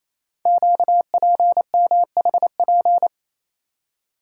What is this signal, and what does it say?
Morse code, heard as a keyed tone, about 27 wpm, 700 Hz. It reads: QPMHP
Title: Code Groups - 5 Letters - 28wpm